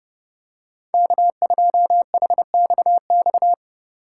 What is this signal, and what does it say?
Morse code, heard as a keyed tone, about 28 wpm, 700 Hz. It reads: K2HXX